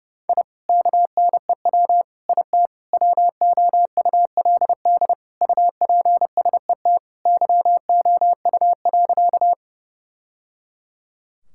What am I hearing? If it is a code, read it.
Morse code, heard as a keyed tone, about 30 wpm, 700 Hz. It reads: I KNEW IT WOULD UPSET YOU.
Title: Sentences from Top 1000 Words - Review Part 5 - 30wpm